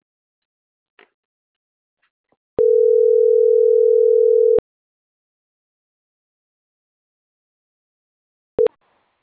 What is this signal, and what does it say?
Ringback tone (North America), 440 + 480 Hz.